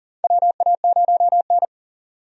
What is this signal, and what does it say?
Morse code, heard as a keyed tone, about 40 wpm, 700 Hz. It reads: WA0N
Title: USA Call Signs - 4 Characters - 40wpm